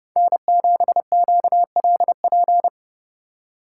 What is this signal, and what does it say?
Morse code, heard as a keyed tone, about 29 wpm, 700 Hz. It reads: N7QLP